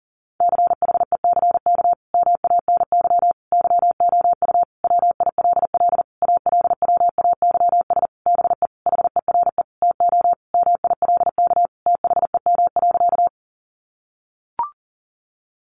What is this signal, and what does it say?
Morse code, heard as a keyed tone, about 37 wpm, 700 Hz. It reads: CHECK MANY YOU WILL ALWAYS BE HERE TO MILK THEM.
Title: Sentences from Top 1500 Words 40wpm